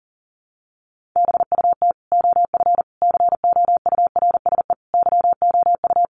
Morse code, heard as a keyed tone, about 40 wpm, 700 Hz. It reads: BUT OF COURSE YOU
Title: Sentences from Top 1500 Words 40wpm